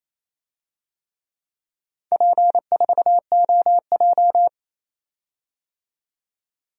Morse code, heard as a keyed tone, about 27 wpm, 700 Hz. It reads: P4OJ
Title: DXCC-100 Call Signs - 4 Characters - 28wpm